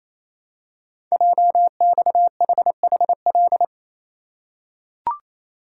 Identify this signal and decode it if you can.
Morse code, heard as a keyed tone, about 29 wpm, 700 Hz. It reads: JXHHL
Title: Code Groups - 5 Letters - 28wpm